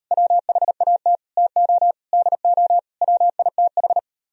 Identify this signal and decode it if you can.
Morse code, heard as a keyed tone, about 38 wpm, 700 Hz. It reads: WHAT TO DO WITH